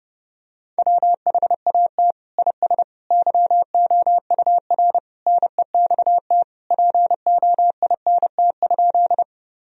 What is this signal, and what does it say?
Morse code, heard as a keyed tone, about 30 wpm, 700 Hz. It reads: WHAT IS YOUR NEXT POINT?
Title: Sentences from Top 1000 Words - Review Part 1 - 30wpm